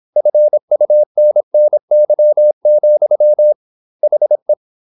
Morse code, heard as a keyed tone, about 27 wpm, 600 Hz. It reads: FUNNY, HE
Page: Never 26wpm | CW med Gnister